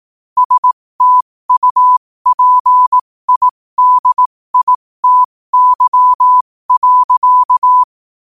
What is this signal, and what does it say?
Morse code, heard as a keyed tone, about 18 wpm, spaced as for 15 wpm, 1000 Hz. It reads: STUPIDITY.
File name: 15wpm